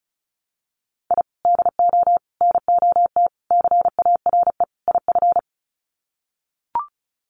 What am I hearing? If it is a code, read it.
Morse code, heard as a keyed tone, about 34 wpm, 700 Hz. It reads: I DO NOT CARE IF
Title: Sets of 5 Words 35wpm